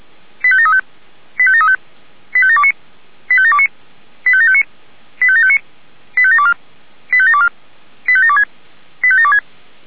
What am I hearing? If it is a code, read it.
Five-tone selective call (ZVEI), heard as quick runs of five tones, tones of about 70 ms. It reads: 85735 85735 85729 85729 85759 85759 85724 85724 85726 85726
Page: /464/SDR/ZVEI/